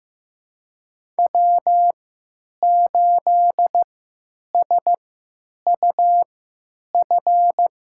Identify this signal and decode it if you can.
Morse code, heard as a keyed tone, about 15 wpm, spaced as for 9 wpm, 700 Hz. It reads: W8SUF
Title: USA Call Signs - 3x Character Spacing - 15wpm